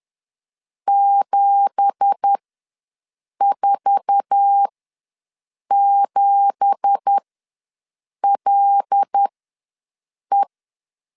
Morse code, heard as a keyed tone, about 10 wpm, spaced as for 6 wpm, 800 Hz. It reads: 747LE
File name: LEZIONI / 9 2867 ws 10wpm 4215